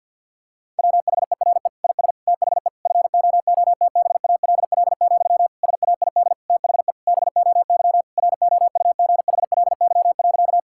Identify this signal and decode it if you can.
Morse code, heard as a keyed tone, about 50 wpm, 700 Hz. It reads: WHERE IS THE FOOTBALL, SAID THE BOY ROUGHLY.
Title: Sentences from Top 1600 Words 50wpm